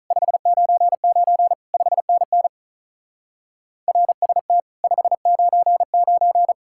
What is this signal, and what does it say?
Morse code, heard as a keyed tone, about 43 wpm, 700 Hz. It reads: 599 5NN RST 599
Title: Speed-Racing Common QSO Phrases Encore - 35wpm